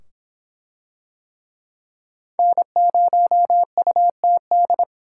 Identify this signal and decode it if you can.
Morse code, heard as a keyed tone, about 23 wpm, 700 Hz. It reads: N0UTD